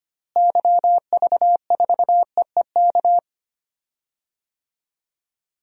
Morse code, heard as a keyed tone, about 25 wpm, 700 Hz. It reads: YV4EEK
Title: DXCC-100 Call Signs - 6 Characters - Repeated 2x - 25wpm